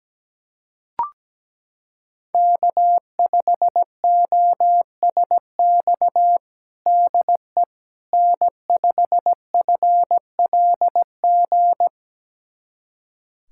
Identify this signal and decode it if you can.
Morse code, heard as a keyed tone, about 17 wpm, 700 Hz. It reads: K5OSX DE N5FLG
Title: USA Callsign DE Callsign - 5 Characters - 17wpm